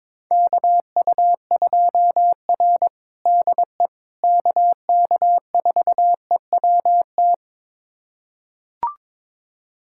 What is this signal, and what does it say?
Morse code, heard as a keyed tone, about 21 wpm, 700 Hz. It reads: KU2R DE KK4EWT
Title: USA Callsign DE Callsign 22wpm